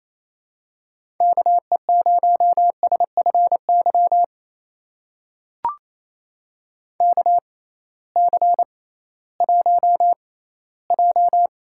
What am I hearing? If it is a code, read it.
Morse code, heard as a keyed tone, about 28 wpm, spaced as for 16 wpm, 700 Hz. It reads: KE0SFY KC1J
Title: USA Call Signs - 6 Characters - 6x Character Spacing - 28wpm